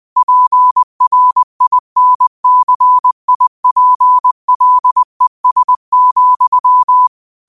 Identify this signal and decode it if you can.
Morse code, heard as a keyed tone, about 20 wpm, 1000 Hz. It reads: PRINCIPLES,
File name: qotdfiles / 20wpm